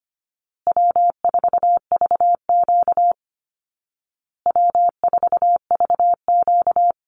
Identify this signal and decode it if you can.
Morse code, heard as a keyed tone, about 25 wpm, 700 Hz. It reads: W4VQ W4VQ
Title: Call Signs Repeated 2x - 25wpm